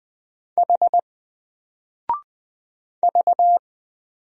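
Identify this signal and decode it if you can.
Morse code, heard as a keyed tone, about 20 wpm, 700 Hz. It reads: H V
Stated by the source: Single letter and number Morse drill at 20 wpm, from N4LCD.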